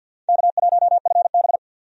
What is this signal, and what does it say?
Morse code, heard as a keyed tone, about 47 wpm, 700 Hz. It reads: K1FB
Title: USA Call Signs - 4 Characters - 50wpm